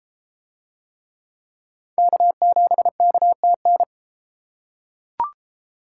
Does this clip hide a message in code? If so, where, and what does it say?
Yes: Morse code from 2.0 to 3.8 s, heard as a keyed tone, about 33 wpm, 700 Hz. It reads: K7KTN